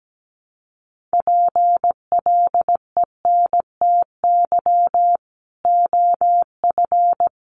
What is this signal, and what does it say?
Morse code, heard as a keyed tone, about 17 wpm, 700 Hz. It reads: PLENTY OF
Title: Sentences from Top 1500 Words 17wpm